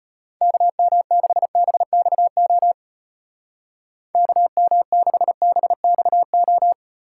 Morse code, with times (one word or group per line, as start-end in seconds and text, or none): 0.4-2.7 KM6BXO
4.1-6.7 KM6BXO